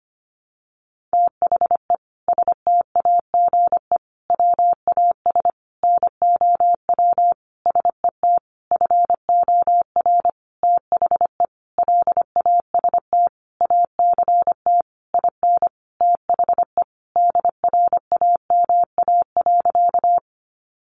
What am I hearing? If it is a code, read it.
Morse code, heard as a keyed tone, about 25 wpm, 700 Hz. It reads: THE STAGE WAS NOW SET FOR THE LAST ACT IN THE DRAMA.
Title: Sentences from Top 1500 Words 25wpm